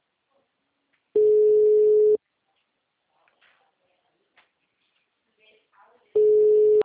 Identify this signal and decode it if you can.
Ringback tone (Europe, 425 Hz), with speech.